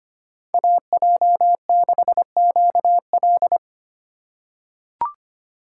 Morse code, heard as a keyed tone, about 25 wpm, 700 Hz. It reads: AJ6QL
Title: K1USN SST QSO Practice - 25wpm